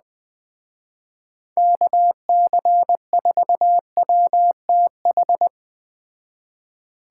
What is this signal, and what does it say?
Morse code, heard as a keyed tone, about 21 wpm, 700 Hz. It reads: KC4WTH
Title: Speed Racing - USA 2x3 Call Signs 17wpm